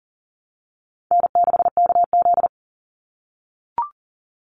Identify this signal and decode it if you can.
Morse code, heard as a keyed tone, about 40 wpm, 700 Hz. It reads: N6KZ
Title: NA SOTA Chaser Callsigns 40wpm